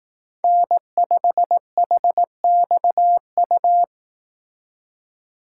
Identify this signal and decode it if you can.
Morse code, heard as a keyed tone, about 18 wpm, 700 Hz. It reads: N5HXU
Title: Speed Racing - USA Call Signs 15wpm